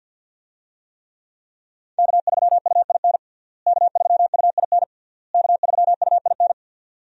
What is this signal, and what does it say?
Morse code, heard as a keyed tone, about 52 wpm, 700 Hz. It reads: K3UIN K3UIN K3UIN
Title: USA Call Signs - 5 Characters - Repeated 3x - 50wpm